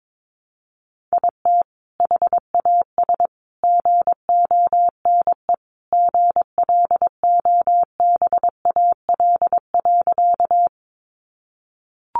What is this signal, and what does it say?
Morse code, heard as a keyed tone, about 22 wpm, 700 Hz. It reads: IT HAS GONE GLOBAL.
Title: Sentences from Top 1200 Words 22wpm